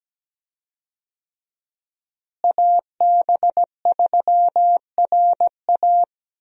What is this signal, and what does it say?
Morse code, heard as a keyed tone, about 17 wpm, 700 Hz. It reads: AB3RA
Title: USA Call Signs - 5 Characters - Repeated 3x - 17wpm